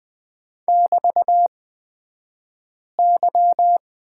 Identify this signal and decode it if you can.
Morse code, heard as a keyed tone, about 20 wpm, spaced as for 5 wpm, 700 Hz. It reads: <BT>Y
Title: 242 Lesson 39d - Letter z - Two Characters 20wpm